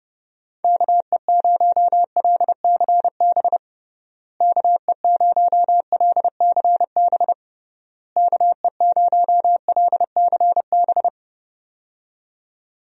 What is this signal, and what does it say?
Morse code, heard as a keyed tone, about 30 wpm, 700 Hz. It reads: KE0LCB KE0LCB KE0LCB
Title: USA Call Signs - 6 Characters - Repeated 3x - 30wpm